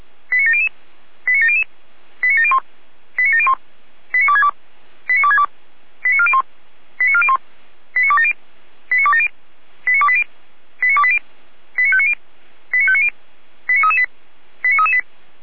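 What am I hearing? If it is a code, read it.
Five-tone selective call (ZVEI), heard as quick runs of five tones, tones of about 70 ms. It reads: 89700 89700 89701 89701 89262 89262 89401 89401 89270 89270 89280 89280 89590 89590 89338 89338
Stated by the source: ZVEI five-tone selective call received on an SDR.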